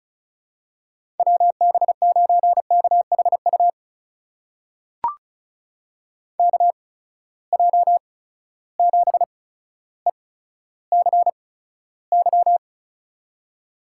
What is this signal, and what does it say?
Morse code, heard as a keyed tone, about 36 wpm, spaced as for 15 wpm, 700 Hz. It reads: WB9KHU KJ7ECY
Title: USA Call Signs - 6 Characters - 8x Character Spacing - 35wpm